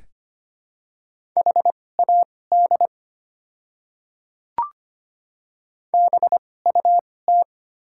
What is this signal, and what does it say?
Morse code, heard as a keyed tone, about 25 wpm, spaced as for 18 wpm, 690 Hz. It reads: HAD BUT
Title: 2x Character Spacing - Top 100 Words - 25wpm